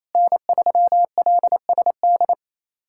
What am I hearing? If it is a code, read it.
Morse code, heard as a keyed tone, about 27 wpm, 700 Hz. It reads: N3LSD
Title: USA Callsign DE Callsign - 5 Characters - 28wpm